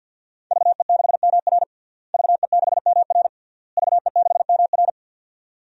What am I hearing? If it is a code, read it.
Morse code, heard as a keyed tone, about 50 wpm, 700 Hz. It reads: VE6MR VE6MR VE6MR
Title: DXCC-100 Call Signs - 5 Characters - Repeated 3x - 50wpm